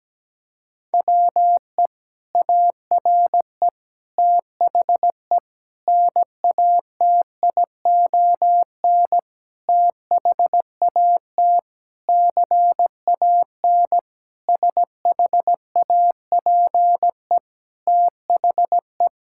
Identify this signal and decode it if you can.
Morse code, heard as a keyed tone, about 17 wpm, 700 Hz. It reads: WE ARE THE NATION THAT CAN SHAPE THE
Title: Sentences from Top 1000 Words - Review Part 6 - 17wpm